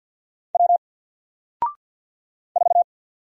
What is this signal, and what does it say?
Morse code, heard as a keyed tone, about 54 wpm, 700 Hz. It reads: W 4
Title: Single Letter-Number Alt 50wpm